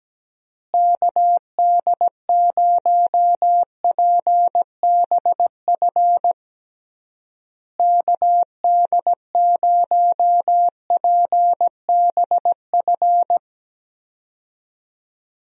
Morse code, heard as a keyed tone, about 17 wpm, 700 Hz. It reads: KD0PBF KD0PBF
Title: USA Call Signs - 6 Characters - Repeated 3x - 17wpm